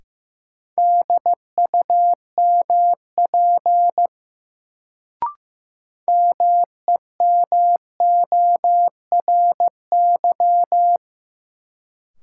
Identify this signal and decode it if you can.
Morse code, heard as a keyed tone, about 15 wpm, 700 Hz. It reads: DUMP MEMORY
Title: Sentences from Top 1300 Words - New Word Drill - 15wpm